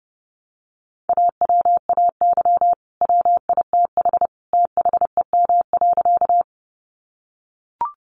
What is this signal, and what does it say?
Morse code, heard as a keyed tone, about 30 wpm, 700 Hz. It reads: AWAY WITH THEM.
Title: Sentences from Top 600 Words 30wpm